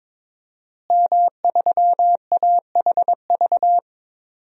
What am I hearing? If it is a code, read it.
Morse code, heard as a keyed tone, about 22 wpm, 700 Hz. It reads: M3AHV
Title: DXCC-100 Call Signs - 5 Characters - Repeated 2x - 22wpm